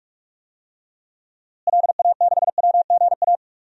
Morse code, heard as a keyed tone, about 46 wpm, 700 Hz. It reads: LA6WGA
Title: DXCC-100 Call Signs - 6 Characters - Repeated 2x - 45wpm